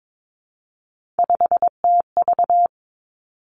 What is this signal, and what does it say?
Morse code, heard as a keyed tone, about 22 wpm, 700 Hz. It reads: 5TV
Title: Three Challenging Letter-Number Combinations 22wpm